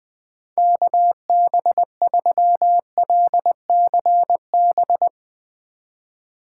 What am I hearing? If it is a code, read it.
Morse code, heard as a keyed tone, about 20 wpm, 700 Hz. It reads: KB3LCB
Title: USA Call Signs - 3x Character Spacing - 20wpm